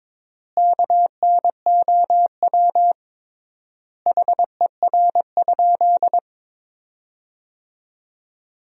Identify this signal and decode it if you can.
Morse code, heard as a keyed tone, about 23 wpm, 700 Hz. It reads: KNOW HER?
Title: 3x Word-Spacing - Sentences from Top 100 Words - 22wpm